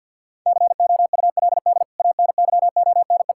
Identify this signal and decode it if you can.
Morse code, heard as a keyed tone, about 50 wpm, 700 Hz. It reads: COULD ANYONE
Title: Sentences from Top 1600 Words 50wpm